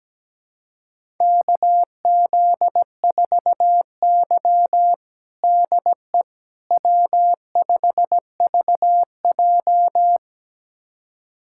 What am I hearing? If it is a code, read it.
Morse code, heard as a keyed tone, about 17 wpm, 700 Hz. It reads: KZ4Y DE W5VJ
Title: USA Callsign DE Callsign - 4 Characters - 17wpm